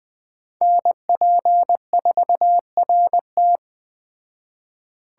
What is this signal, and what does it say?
Morse code, heard as a keyed tone, about 20 wpm, 700 Hz. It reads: NP4RT